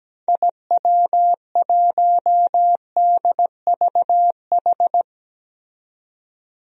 Morse code, heard as a keyed tone, about 17 wpm, 700 Hz. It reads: IW1DVH
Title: DXCC-100 Call Signs - 6 Characters - 17wpm